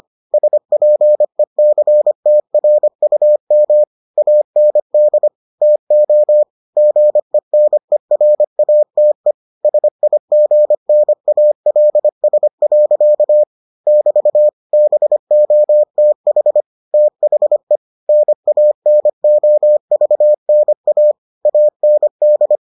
Morse code, heard as a keyed tone, about 25 wpm, 600 Hz. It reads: SPECTRUM AND TO GENERATE SIGNALS. <BT> BOTH THE NANOVNA AND